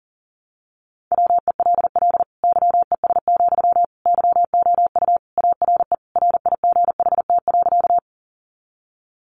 Morse code, heard as a keyed tone, about 40 wpm, 700 Hz. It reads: WELL YES, YOU ARE RIGHT.
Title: Sentences from Top 500 Words 40wpm